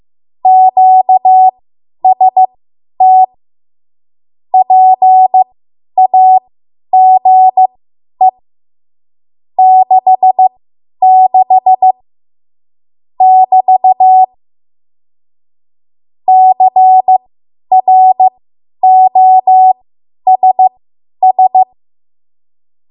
Morse code, heard as a keyed tone, about 15 wpm, spaced as for 10 wpm, 750 Hz. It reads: QST PAGE 66 <BT> CROSS
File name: Morse / 10WPM